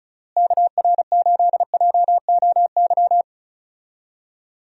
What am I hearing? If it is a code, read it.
Morse code, heard as a keyed tone, about 36 wpm, 700 Hz. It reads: KR8JOY
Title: USA Call Signs - 6 Characters - Repeated 3x - 35wpm